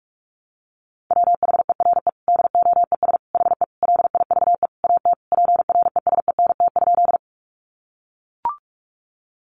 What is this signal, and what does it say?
Morse code, heard as a keyed tone, about 44 wpm, 700 Hz. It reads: WHERE DOES HE LIVE AT PRESENT?
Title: Sentences from Top 800 Words 45wpm